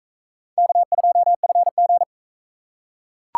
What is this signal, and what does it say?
Morse code, heard as a keyed tone, about 38 wpm, 700 Hz. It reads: K2FG